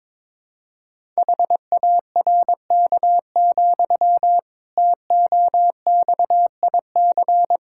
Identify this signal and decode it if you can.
Morse code, heard as a keyed tone, about 23 wpm, 700 Hz. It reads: HARK, TOXIC
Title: Pangram Challenge v2 - 22wpm